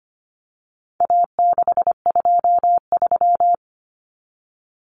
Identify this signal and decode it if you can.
Morse code, heard as a keyed tone, about 25 wpm, 700 Hz. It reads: A623
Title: Common Cut Numbers - 4-Digit 25wpm